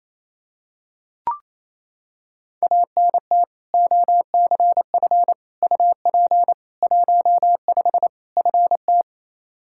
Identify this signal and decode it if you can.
Morse code, heard as a keyed tone, about 27 wpm, 700 Hz. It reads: ANT OCF UP 15 FT
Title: Practice QSO - Exchange 2 - Info - 28wpm